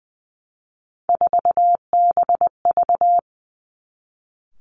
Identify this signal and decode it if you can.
Morse code, heard as a keyed tone, about 20 wpm, 700 Hz. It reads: 4BV
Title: Three Challenging Letter-Number Combinations 20wpm